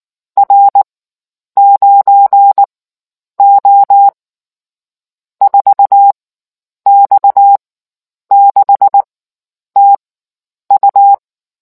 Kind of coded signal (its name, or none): Morse code